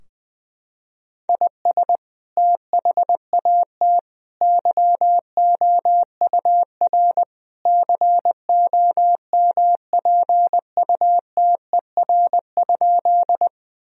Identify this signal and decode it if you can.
Morse code, heard as a keyed tone, about 21 wpm, 700 Hz. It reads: IS THAT YOUR COMPUTER?
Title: Sentences from Top 1000 Words - Review Part 3 - 20wpm